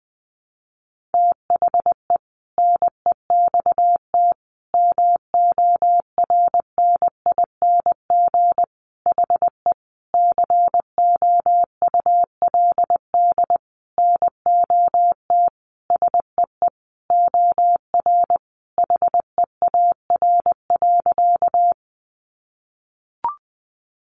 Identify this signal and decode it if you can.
Morse code, heard as a keyed tone, about 20 wpm, 700 Hz. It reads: THE NEXT MORNING HE COULD NOT SEE OR HEAR.
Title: Sentences from Top 600 Words 20wpm